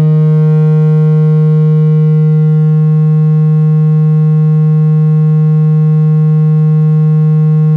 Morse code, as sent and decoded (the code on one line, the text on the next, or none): none